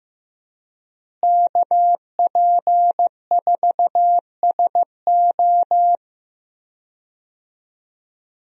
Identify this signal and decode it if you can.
Morse code, heard as a keyed tone, about 15 wpm, 700 Hz. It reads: KP4SO
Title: DXCC-100 Call Signs - 5 Characters - Repeated 2x - 15wpm